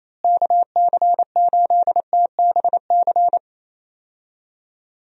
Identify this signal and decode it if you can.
Morse code, heard as a keyed tone, about 28 wpm, 700 Hz. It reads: KC8TBC
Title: USA Call Signs - 6 Characters - Repeated 2x - 28wpm